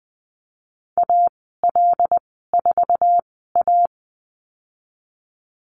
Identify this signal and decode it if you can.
Morse code, heard as a keyed tone, about 20 wpm, spaced as for 14 wpm, 700 Hz. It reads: AL4A
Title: 076 Lesson 13g - Letter L - Call Signs 20wpm